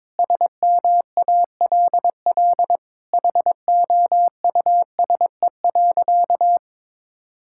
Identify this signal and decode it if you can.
Morse code, heard as a keyed tone, about 22 wpm, 700 Hz. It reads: SMALL HOUSE.
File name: practice / 22wpm / Sentences with 8-Word Phrases - 22wpm